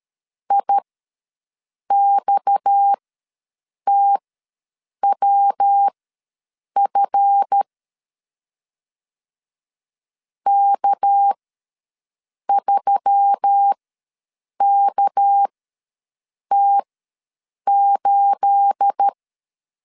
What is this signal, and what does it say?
Morse code, heard as a keyed tone, about 13 wpm, spaced as for 7 wpm, 800 Hz. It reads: IXTWF K3KT8
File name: LEZIONI / 9 28 ws 12wpm 3354